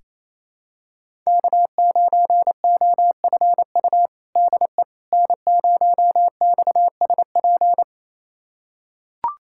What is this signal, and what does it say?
Morse code, heard as a keyed tone, about 28 wpm, 700 Hz. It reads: K9OFU DE N0XSP
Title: USA Callsign DE Callsign - 5 Characters - 28wpm